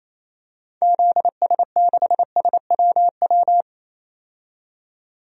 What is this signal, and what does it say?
Morse code, heard as a keyed tone, about 28 wpm, 700 Hz. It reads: ZS6SWW
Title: DXCC-100 Call Signs - 6 Characters - 28wpm